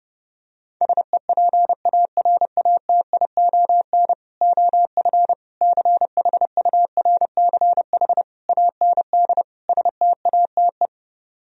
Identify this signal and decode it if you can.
Morse code, heard as a keyed tone, about 32 wpm, 700 Hz. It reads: SEPARATION OF CHURCH AND STATE